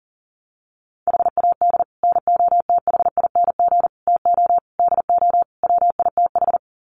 Morse code, heard as a keyed tone, about 40 wpm, 700 Hz. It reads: HAD NOTHING TO DO WITH
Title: Sets of 5 Words 40wpm